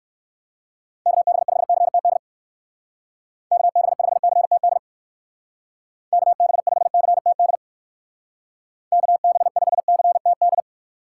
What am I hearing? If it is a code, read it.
Morse code, heard as a keyed tone, about 66 wpm, 700 Hz. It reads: KB5CTD KB5CTD KB5CTD KB5CTD